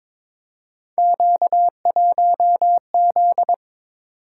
Morse code, heard as a keyed tone, about 22 wpm, 700 Hz. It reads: Q1Z
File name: practice / 22wpm / Three Letter-Number Combinations - 22wpm